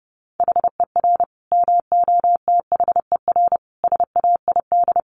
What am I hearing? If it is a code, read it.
Morse code, heard as a keyed tone, about 30 wpm, 700 Hz. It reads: HER MOTHER SAID
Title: Sentences from Top 1200 Words 30wpm